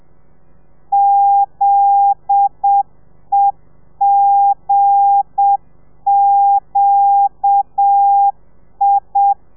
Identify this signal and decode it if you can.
Morse code, heard as a keyed tone, about 7 wpm, 800 Hz. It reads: ZEGQI